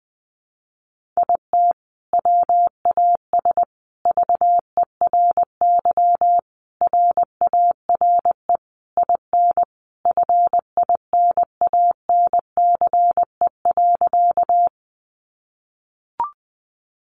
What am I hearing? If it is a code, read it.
Morse code, heard as a keyed tone, about 20 wpm, 700 Hz. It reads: IT WAS VERY RARE IN FINANCE.
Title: Sentences from Top 1500 Words 20wpm